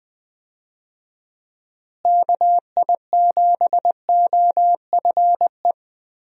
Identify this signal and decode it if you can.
Morse code, heard as a keyed tone, about 20 wpm, 700 Hz. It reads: KI7OFE
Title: USA Call Signs - 6 Characters - Repeated 3x - 20wpm